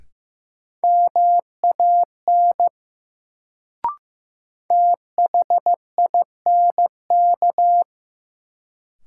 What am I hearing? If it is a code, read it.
Morse code, heard as a keyed tone, about 16 wpm, 700 Hz. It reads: MAN THINK
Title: Sentences from Top 200 Words - New Word Drill - 15wpm